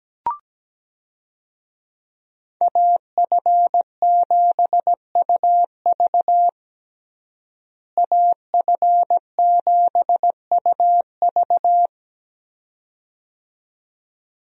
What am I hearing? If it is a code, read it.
Morse code, heard as a keyed tone, about 17 wpm, 700 Hz. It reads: AF7UV AF7UV
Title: USA Call Signs - 5 Characters - Repeated 2x - 17wpm